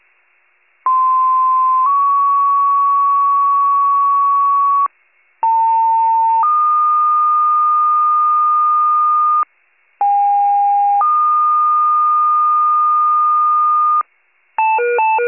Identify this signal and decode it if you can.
Two-tone page, heard as a one-second tone, then a three-second tone, about 1040 then 1120 Hz, about 900 then 1220 Hz, about 810 then 1220 Hz. It is heard from 0.9 to 14.0 s.
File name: wcalert